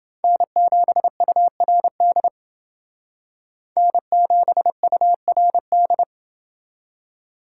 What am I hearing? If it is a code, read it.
Morse code, heard as a keyed tone, about 28 wpm, 700 Hz. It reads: N7URD N7URD